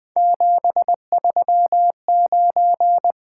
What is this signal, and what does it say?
Morse code, heard as a keyed tone, about 20 wpm, 700 Hz. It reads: 739
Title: Three Letter-Number Combinations - 20wpm